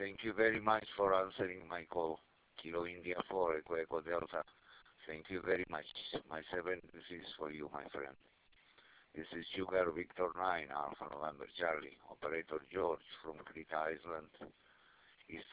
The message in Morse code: none